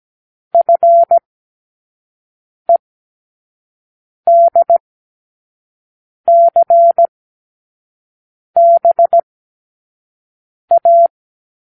Morse code, heard as a keyed tone, about 17 wpm, spaced as for 5 wpm, 680 Hz. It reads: FEDCBA
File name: k7qo / Lesson9(A-F)